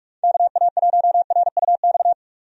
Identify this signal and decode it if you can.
Morse code, heard as a keyed tone, about 46 wpm, 700 Hz. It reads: KA1RUX
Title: USA Call Signs - 6 Characters - 45wpm